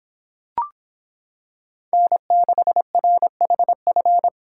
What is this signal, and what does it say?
Morse code, heard as a keyed tone, about 25 wpm, 700 Hz. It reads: N6RHF